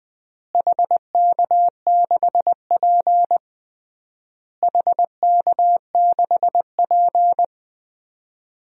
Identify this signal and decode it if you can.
Morse code, heard as a keyed tone, about 20 wpm, 700 Hz. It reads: HK6P HK6P
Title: DXCC-100 Call Signs - 4 Characters - Repeated 3x - 20wpm